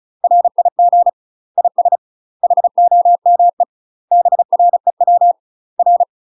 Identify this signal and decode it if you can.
Morse code, heard as a keyed tone, about 34 wpm, 700 Hz. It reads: RIG IS HOME BREW R
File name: UK MASTER QSOs TRAINING / 35WPM / UK MASTER QSOs 0030